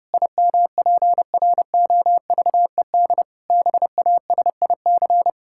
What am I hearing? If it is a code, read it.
Morse code, heard as a keyed tone, about 30 wpm, 700 Hz. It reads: IMPROVED BASIC
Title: Pangram Challenge v2 - 30wpm